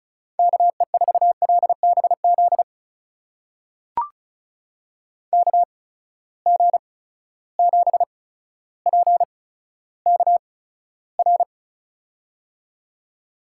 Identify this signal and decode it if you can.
Morse code, heard as a keyed tone, about 34 wpm, spaced as for 15 wpm, 700 Hz. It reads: KE4LBZ KG7PKR
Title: USA Call Signs - 6 Characters - 8x Character Spacing - 35wpm